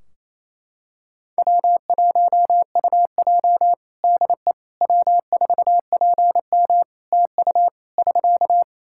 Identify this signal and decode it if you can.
Morse code, heard as a keyed tone, about 28 wpm, 700 Hz. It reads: W1UJ DE W4PM TU <SK>
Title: K1USN SST QSO Practice 28wpm